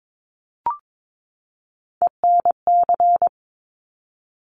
Morse code, heard as a keyed tone, about 22 wpm, 700 Hz. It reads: ENC
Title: Top Three Letter Combinations in Words 22wpm